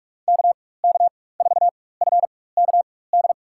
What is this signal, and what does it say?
Morse code, heard as a keyed tone, about 43 wpm, spaced as for 21 wpm, 700 Hz. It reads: KK4FKD
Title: USA Call Signs - 4x Character Spacing - 45wpm